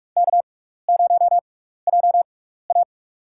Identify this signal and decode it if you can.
Morse code, heard as a keyed tone, about 47 wpm, spaced as for 16 wpm, 700 Hz. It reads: K0JA
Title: USA Call Signs - 4 Characters - 6x Character Spacing - 45wpm